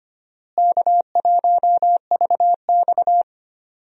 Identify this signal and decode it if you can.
Morse code, heard as a keyed tone, about 25 wpm, 700 Hz. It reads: K1VX
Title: USA Call Signs - 4 Characters - Repeated 3x - 25wpm